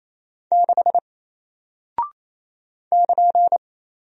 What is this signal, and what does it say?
Morse code, heard as a keyed tone, about 28 wpm, 700 Hz. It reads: 6 <KN>
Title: Single Character (Letter:Number:Punctuation:Prosign) Alt 28wpm